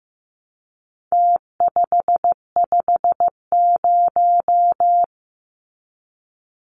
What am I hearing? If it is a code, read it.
Morse code, heard as a keyed tone, about 15 wpm, 700 Hz. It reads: T550
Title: Numbers - Contest Style - 4-Digit 15wpm